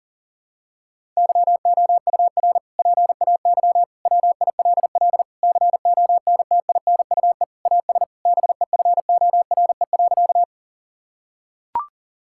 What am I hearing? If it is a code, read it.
Morse code, heard as a keyed tone, about 40 wpm, 700 Hz. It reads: YOUR PAY WILL CONTINUE AS BEFORE.
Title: Sentences from Top 1000 Words - Review Part 6 - 40wpm